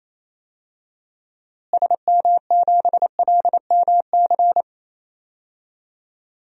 Morse code, heard as a keyed tone, about 28 wpm, 700 Hz. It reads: SM7LMC
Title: DXCC-100 Call Signs - 6 Characters - Repeated 2x - 28wpm